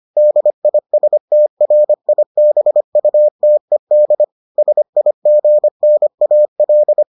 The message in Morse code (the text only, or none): DISTRIBUTED SIGNAL